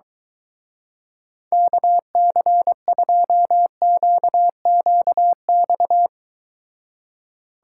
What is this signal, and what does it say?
Morse code, heard as a keyed tone, about 24 wpm, 700 Hz. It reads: KC2QQX